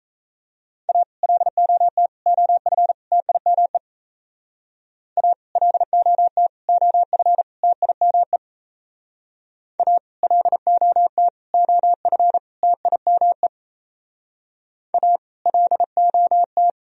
Morse code, heard as a keyed tone, about 34 wpm, 700 Hz. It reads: A LOT OF TIME A LOT OF TIME A LOT OF TIME A LOT
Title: Speed-Racing - Sets of 4 Words 28wpm